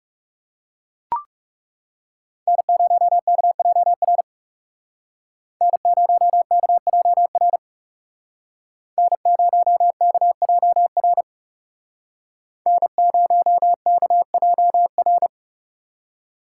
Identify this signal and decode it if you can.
Morse code, heard as a keyed tone, about 37 wpm, 700 Hz. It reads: N0KJR N0KJR N0KJR N0KJR
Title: Speed Racing - USA 1x3 Call Signs 30wpm